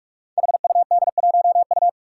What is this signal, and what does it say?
Morse code, heard as a keyed tone, about 44 wpm, 700 Hz. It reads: HUD1U
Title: Five Challenging Letter-Number Combinations 45wpm